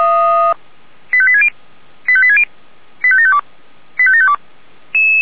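Five-tone selective call (ZVEI), heard as quick runs of five tones, tones of about 70 ms. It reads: 85860 85860 85752 85752
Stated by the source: ZVEI five-tone selective call received on SDR.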